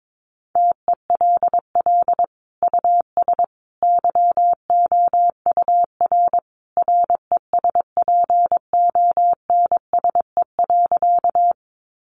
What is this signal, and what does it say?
Morse code, heard as a keyed tone, about 22 wpm, 700 Hz. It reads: TELL US YOUR RESPONSE.
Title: Sentences from Top 1400 Words 22wpm